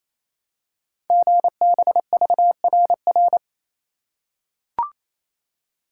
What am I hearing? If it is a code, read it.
Morse code, heard as a keyed tone, about 27 wpm, 700 Hz. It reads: GBVRR
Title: Code Groups - 5 Letters - 28wpm